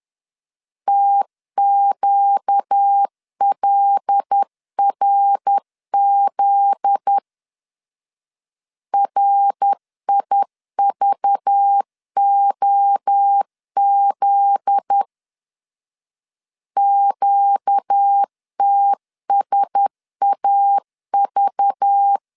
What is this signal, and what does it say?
Morse code, heard as a keyed tone, about 10 wpm, 800 Hz. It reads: TQLRZ RIVOZ QTSAV
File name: LEZIONI / 8 YJ45 / 8 Y 10wpm 2615